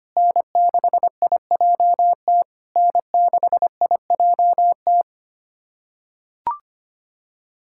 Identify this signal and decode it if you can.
Morse code, heard as a keyed tone, about 25 wpm, 700 Hz. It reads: N6IJT N6IJT
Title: SOTA QSO Practice 25wpm